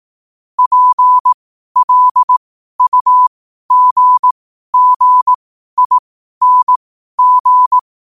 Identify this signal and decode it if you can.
Morse code, heard as a keyed tone, about 18 wpm, spaced as for 12 wpm, 1000 Hz. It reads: PLUGGING
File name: qotdfiles / 12wpm